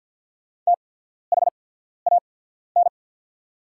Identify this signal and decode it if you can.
Morse code, heard as a keyed tone, about 50 wpm, spaced as for 14 wpm, 700 Hz. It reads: THAN
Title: 8x Character Spacing - Top 100 Words - 50wpm